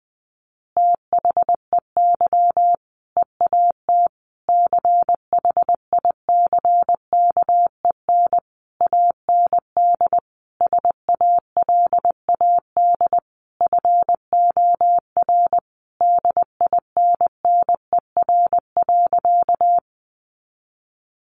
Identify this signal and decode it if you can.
Morse code, heard as a keyed tone, about 20 wpm, 700 Hz. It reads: THEY EAT CHICKEN AND SALAD FOR DINNER.
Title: Easy Phrases 20wpm